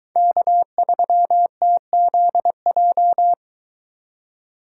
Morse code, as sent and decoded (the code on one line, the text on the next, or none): -.- ...-- - --.. .---
K3TZJ